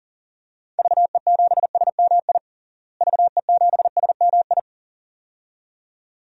Morse code, heard as a keyed tone, about 40 wpm, 700 Hz. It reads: VE7SMI VE7SMI
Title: DXCC-100 Call Signs - 6 Characters - Repeated 2x - 40wpm